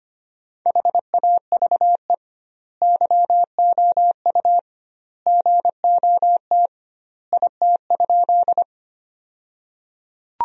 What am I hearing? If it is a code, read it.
Morse code, heard as a keyed tone, about 25 wpm, 700 Hz. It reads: HAVE YOU GOT IT?
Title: 2x Word-Spacing - Sentences from Top 300 Words - 25wpm